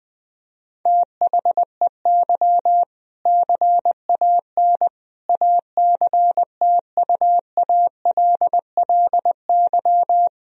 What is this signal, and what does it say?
Morse code, heard as a keyed tone, about 20 wpm, 700 Hz. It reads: THEY CAN ACTUALLY
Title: Sentences from Top 1000 Words - Review Part 4 - 20wpm